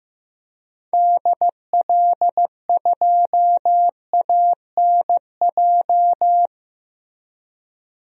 Morse code, heard as a keyed tone, about 15 wpm, 700 Hz. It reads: DL2ANJ
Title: DXCC-100 Call Signs - 6 Characters - Repeated 3x - 15wpm